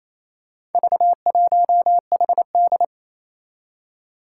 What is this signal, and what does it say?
Morse code, heard as a keyed tone, about 27 wpm, 700 Hz. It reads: V1HD
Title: Four Challenging Letter-Number Combinations 28wpm